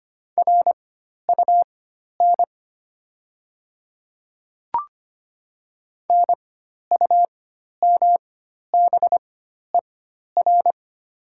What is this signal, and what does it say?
Morse code, heard as a keyed tone, about 25 wpm, spaced as for 12 wpm, 700 Hz. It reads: RUN NUMBER
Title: 4x Character Spacing - Top 300 Words - 25wpm